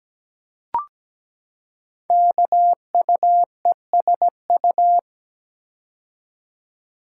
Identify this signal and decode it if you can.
Morse code, heard as a keyed tone, about 17 wpm, 700 Hz. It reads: KUESU
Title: Code Groups - 5 Letter:Numbers - 17wpm